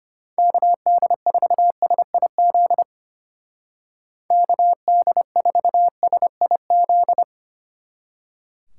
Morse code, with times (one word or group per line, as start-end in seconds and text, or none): 0.4-2.8 KD4SIZ
4.3-7.2 KD4SIZ